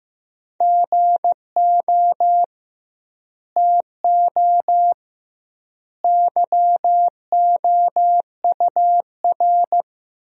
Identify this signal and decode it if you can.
Morse code, heard as a keyed tone, about 15 wpm, 700 Hz. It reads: GO TO YOUR
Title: 2x Word-Spacing - Sentences from Top 300 Words - 15wpm